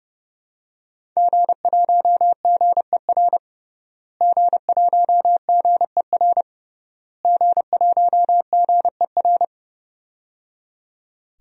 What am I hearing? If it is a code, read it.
Morse code, heard as a keyed tone, about 31 wpm, 700 Hz. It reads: G1GER G1GER G1GER
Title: DXCC-100 Call Signs - 5 Characters - Repeated 3x - 30wpm